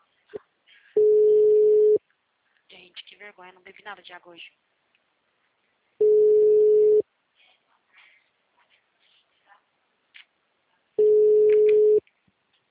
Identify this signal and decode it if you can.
Ringback tone (Europe, 425 Hz), with speech.
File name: out- -2004- - - .71069